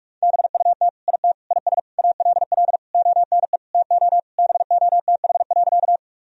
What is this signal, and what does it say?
Morse code, heard as a keyed tone, about 46 wpm, 700 Hz. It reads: BUT IT IS ALL ONE TO BOTH.
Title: Sentences from Top 1000 Words - Review Part 1 - 45wpm